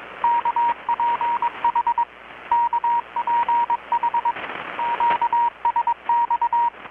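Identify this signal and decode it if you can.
Morse code, heard as a keyed tone, about 21 wpm, 940 Hz, with radio noise. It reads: KPH KPH QSX
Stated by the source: Morse from KPH's Night of Nights XX maritime radiotelegraphy event, received in Chiapas, Mexico about 3,800 km away on a Kenwood R-600 with a grounded 10 m long-wire antenna.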